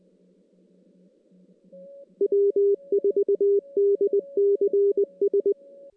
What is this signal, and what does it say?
Morse code, heard as a keyed tone, about 20 wpm, 410 Hz. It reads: W4DCS
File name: 2016 JAN CW NAQP / 2016 Jan naqp k5ot - 0-5